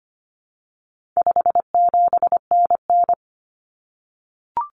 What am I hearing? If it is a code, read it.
Morse code, heard as a keyed tone, about 25 wpm, 700 Hz. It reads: 57NN
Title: Common Cut Numbers - 4-Digit 25wpm